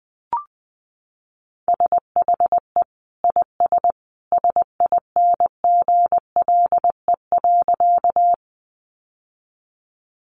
Morse code, heard as a keyed tone, about 20 wpm, 700 Hz. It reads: SHE IS SINGLE.
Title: Sentences from Top 700 Words 20wpm